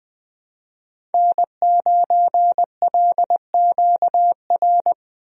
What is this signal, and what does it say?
Morse code, heard as a keyed tone, about 20 wpm, 700 Hz. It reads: N9LQR
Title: USA Call Signs - 5 Characters - 20wpm